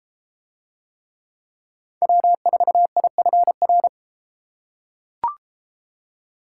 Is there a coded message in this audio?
Yes: Morse code from 2.0 to 3.9 s, heard as a keyed tone, about 31 wpm, 700 Hz. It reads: W4IFR